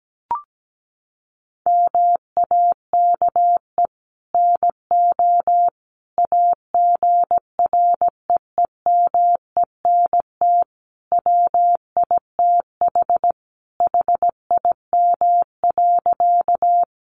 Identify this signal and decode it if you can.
Morse code, heard as a keyed tone, about 17 wpm, 700 Hz. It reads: MAKE NO AGREEMENT WITH HIM.
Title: Sentences from Top 1500 Words 17wpm